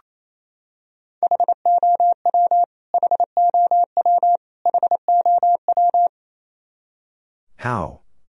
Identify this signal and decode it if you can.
Morse code, heard as a keyed tone, about 28 wpm, 700 Hz. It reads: HOW HOW HOW
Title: Top 100 Words Repeated 28wpm